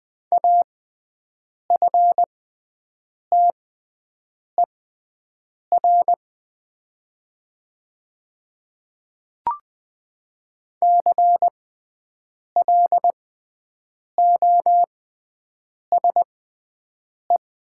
Morse code, heard as a keyed tone, about 20 wpm, spaced as for 7 wpm, 700 Hz. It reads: AFTER CLOSE
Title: 6x Character Spacing - Top 500 Words - 20wpm